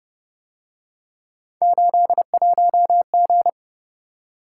Morse code, heard as a keyed tone, about 30 wpm, 700 Hz. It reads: 81G